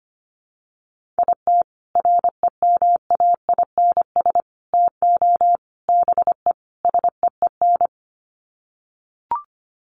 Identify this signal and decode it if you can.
Morse code, heard as a keyed tone, about 25 wpm, 700 Hz. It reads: IT REMAINS TO BE SEEN
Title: Sets of 5 Words Encore 25wpm